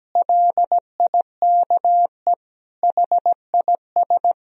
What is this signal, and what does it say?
Morse code, heard as a keyed tone, about 17 wpm, 700 Hz. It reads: LIKE HIS
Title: Sentences from Top 1000 Words - Review Part 3 - 17wpm